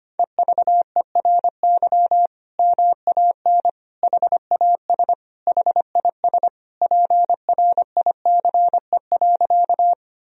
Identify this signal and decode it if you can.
Morse code, heard as a keyed tone, about 25 wpm, 700 Hz. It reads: EVERY MAN HAS HIS PRICE.
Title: Sentences from Top 1000 Words - Review Part 4 - 25wpm